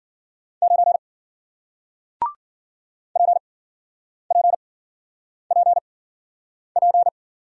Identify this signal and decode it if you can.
Morse code, heard as a keyed tone, about 50 wpm, spaced as for 9 wpm, 700 Hz. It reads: 9 PPPP
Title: Speed-Racing Single Letter-Number 40wpm